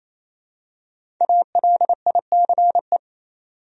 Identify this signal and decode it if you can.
Morse code, heard as a keyed tone, about 27 wpm, 700 Hz. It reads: ALICE
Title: USA Common Names - Max 5 Letters - 28wpm